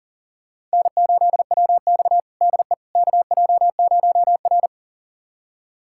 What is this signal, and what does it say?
Morse code, heard as a keyed tone, about 40 wpm, 700 Hz. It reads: N8WX DE KJ0R
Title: USA Callsign DE Callsign - 4 Characters - 40wpm